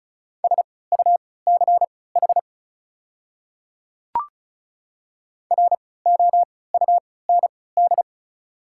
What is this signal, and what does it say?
Morse code, heard as a keyed tone, about 36 wpm, spaced as for 20 wpm, 700 Hz. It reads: SUCH ROUND
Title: 3x Character Spacing - Top 300 Words - 35wpm